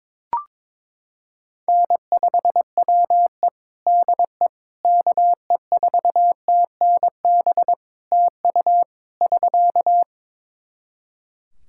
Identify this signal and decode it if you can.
Morse code, heard as a keyed tone, about 22 wpm, 700 Hz. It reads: N5WE DE KE4TNB TU <SK>
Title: K1USN SST QSO Practice 22wpm